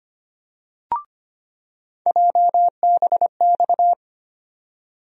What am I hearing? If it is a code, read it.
Morse code, heard as a keyed tone, about 25 wpm, 700 Hz. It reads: JBX
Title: USA Call Sign Postfixes - 3 Letters 25wpm